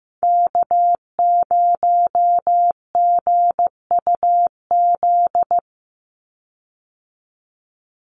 Morse code, heard as a keyed tone, about 15 wpm, 700 Hz. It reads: K0GUZ
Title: Call Signs Repeated 3x - 15wpm